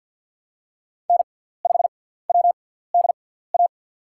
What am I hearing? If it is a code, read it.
Morse code, heard as a keyed tone, about 53 wpm, spaced as for 17 wpm, 700 Hz. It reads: N5WDA
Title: USA Call Signs - 5 Characters - 6x Character Spacing - 50wpm